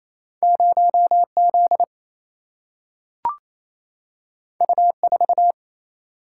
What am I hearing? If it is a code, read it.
Morse code, heard as a keyed tone, about 28 wpm, 700 Hz. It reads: 0 Z U 4
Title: Single-Letter Single-Number 28wpm